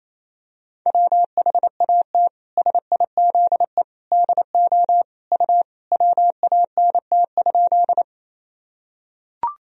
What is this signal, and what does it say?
Morse code, heard as a keyed tone, about 28 wpm, 700 Hz. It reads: WHAT SIZE DO U WANT?
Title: Sentences from Top 400 Words and Abbreviations 28wpm